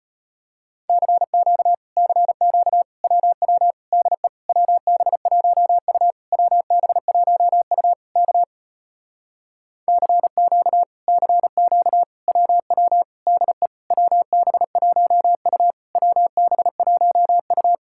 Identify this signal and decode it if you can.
Morse code, heard as a keyed tone, about 33 wpm, 700 Hz. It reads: CQ CQ WW DE WB1U WB1U K CQ CQ WW DE WB1U WB1U